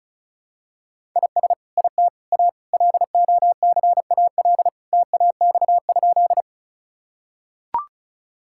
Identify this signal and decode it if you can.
Morse code, heard as a keyed tone, about 36 wpm, 700 Hz. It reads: IS IT A LOCAL TAX?
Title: Sentences from Top 1000 Words - Review Part 5 - 35wpm